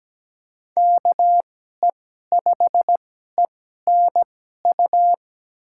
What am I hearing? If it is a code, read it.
Morse code, heard as a keyed tone, about 17 wpm, spaced as for 12 wpm, 700 Hz. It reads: KE5ENU
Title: USA Call Signs - 6 Characters - 2x Character Spacing - 17wpm